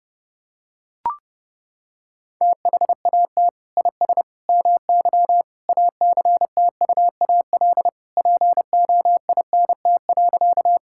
Morse code, heard as a keyed tone, about 30 wpm, 700 Hz. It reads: THAT IS MY ACTUAL POINT.
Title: Sentences from Top 1000 Words - Review Part 5 - 30wpm